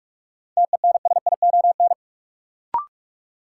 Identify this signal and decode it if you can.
Morse code, heard as a keyed tone, about 46 wpm, 700 Hz. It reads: TENSION
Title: Sentences from Top 1600 Words - New Word Drill Alt - 45wpm